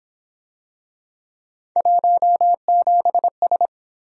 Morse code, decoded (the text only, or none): 17S